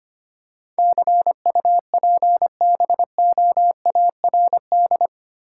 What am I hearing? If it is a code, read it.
Morse code, heard as a keyed tone, about 25 wpm, 700 Hz. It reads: CUPBOARD
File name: Eight Letter Words - 25wpm